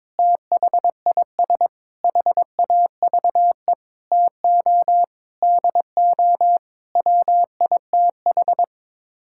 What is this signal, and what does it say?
Morse code, heard as a keyed tone, about 22 wpm, 700 Hz. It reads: THIS HAVE TO DO WITH
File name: practice / 22wpm / Sets of 8 Words - 22wpm